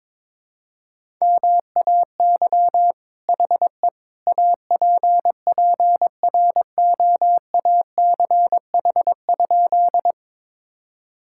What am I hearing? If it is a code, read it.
Morse code, heard as a keyed tone, about 22 wpm, 700 Hz. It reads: MAY HE APPROACH?
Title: Sentences from Top 1000 Words - Review Part 6 - 22wpm